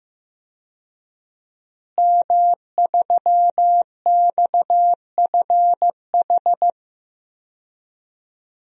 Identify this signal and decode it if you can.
Morse code, heard as a keyed tone, about 15 wpm, 700 Hz. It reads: M3XFH
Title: DXCC-100 Call Signs - 5 Characters - Repeated 2x - 15wpm